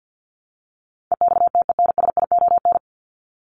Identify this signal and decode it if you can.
Morse code, heard as a keyed tone, about 50 wpm, 700 Hz. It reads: EXTENSION
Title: Top 1501-1750 Words 50wpm